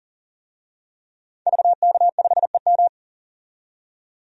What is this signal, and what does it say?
Morse code, heard as a keyed tone, about 40 wpm, 700 Hz. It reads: VK5EM
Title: DXCC-100 Call Signs - 5 Characters - 40wpm